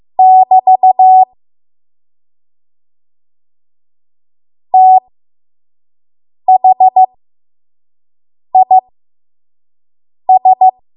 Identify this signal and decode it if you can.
Morse code, heard as a keyed tone, about 15 wpm, spaced as for 5 wpm, 750 Hz. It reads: <BT> THIS
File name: Archive / 5 WPM / 05WPM